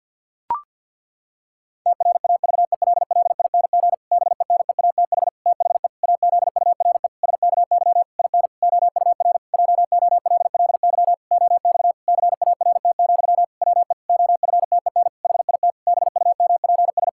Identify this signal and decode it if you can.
Morse code, heard as a keyed tone, about 50 wpm, 700 Hz. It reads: TRAVELLING BENEATH THE AZURE SKY IN OUR JOLLY OX CART, WE OFTEN HIT BUMPS
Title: Pangram Challenge v2 - 50wpm